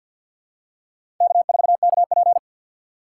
Morse code, heard as a keyed tone, about 52 wpm, 700 Hz. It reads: K4KP
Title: USA Call Signs - 2x Character Spacing - 50wpm